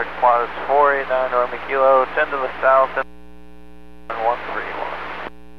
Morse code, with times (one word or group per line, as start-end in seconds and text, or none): none